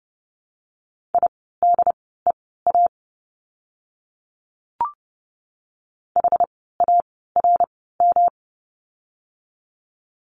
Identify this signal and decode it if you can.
Morse code, heard as a keyed tone, about 30 wpm, spaced as for 17 wpm, 700 Hz. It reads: IDEA HARM
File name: course / 30:10wpm / 108 Lesson 18f - Letter m - Words 30wpm